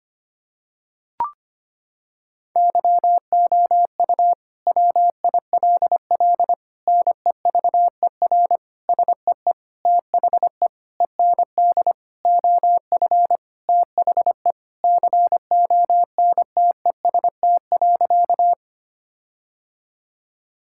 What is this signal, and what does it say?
Morse code, heard as a keyed tone, about 25 wpm, 700 Hz. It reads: YOU WILL NEVER SEE THE END OF THE CONTEST.
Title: Sentences from Top 1600 Words 25wpm